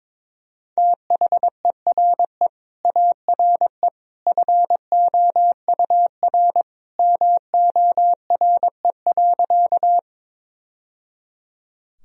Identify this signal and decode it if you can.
Morse code, heard as a keyed tone, about 22 wpm, 700 Hz. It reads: THERE ARE FOUR MORE.
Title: Sentences from Top 1000 Words - Review Part 1 - 22wpm